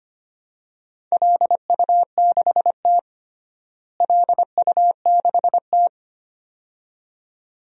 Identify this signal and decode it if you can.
Morse code, heard as a keyed tone, about 25 wpm, 700 Hz. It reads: LU6T LU6T
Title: DXCC-100 Call Signs - 4 Characters - Repeated 2x - 25wpm